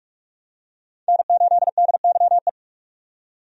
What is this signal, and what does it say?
Morse code, heard as a keyed tone, about 47 wpm, 700 Hz. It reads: N8DYE
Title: USA Call Signs - 5 Characters - 45wpm